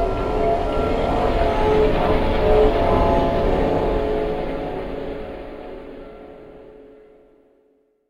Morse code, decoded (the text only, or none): none